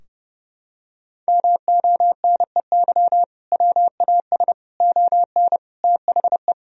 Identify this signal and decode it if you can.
Morse code, heard as a keyed tone, about 30 wpm, 700 Hz. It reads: MONEY WAS ON THE
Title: Sentences from Top 1000 Words - Review Part 2 - 30wpm